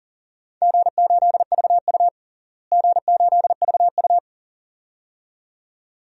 Morse code, heard as a keyed tone, about 40 wpm, 700 Hz. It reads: G8VU G8VU
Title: DXCC-100 Call Signs - 4 Characters - Repeated 2x - 40wpm